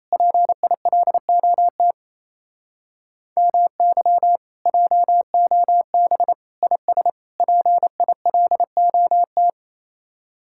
Morse code, heard as a keyed tone, about 35 wpm, 700 Hz. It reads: PILOT MY JOB IS PILOT